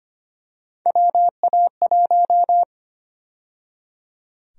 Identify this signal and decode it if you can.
Morse code, heard as a keyed tone, about 25 wpm, 700 Hz. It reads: WA1
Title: USA Call Sign Prefixes - 25wpm